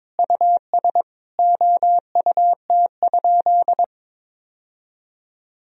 Morse code, heard as a keyed tone, about 22 wpm, 700 Hz. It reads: US OUT?
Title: Sentences from Top 1000 Words - Review Part 3 - 22wpm